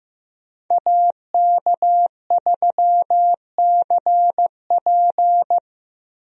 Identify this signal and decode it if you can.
Morse code, heard as a keyed tone, about 15 wpm, 700 Hz. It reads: AK3CP
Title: USA Call Signs - 5 Characters - Repeated 2x - 15wpm